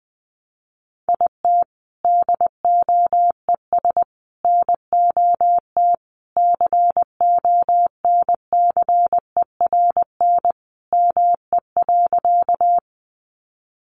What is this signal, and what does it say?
Morse code, heard as a keyed tone, about 20 wpm, 700 Hz. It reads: IT DOES NOT CONCERN ME.
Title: Sentences from Top 1000 Words 20wpm